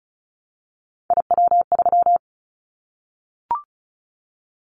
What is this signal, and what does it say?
Morse code, heard as a keyed tone, about 36 wpm, 700 Hz. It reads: IW3
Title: Callsign Prefixes for Top 50 DXCC - 35wpm